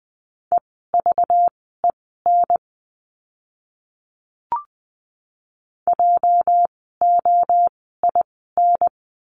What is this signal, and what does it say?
Morse code, heard as a keyed tone, about 20 wpm, spaced as for 14 wpm, 700 Hz. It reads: EVEN JOIN
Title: 206 Lesson 33f - Letter j - Words 20wpm